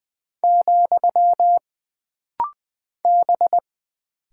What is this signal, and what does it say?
Morse code, heard as a keyed tone, about 20 wpm, 700 Hz. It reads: , B
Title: Single Character (Letter:Number:Punctuation:Prosign) - Rapid-Fire - 20wpm